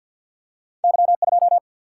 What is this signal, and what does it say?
Morse code, heard as a keyed tone, about 48 wpm, 700 Hz. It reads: Y 2
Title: Single-Letter Single-Number 50wpm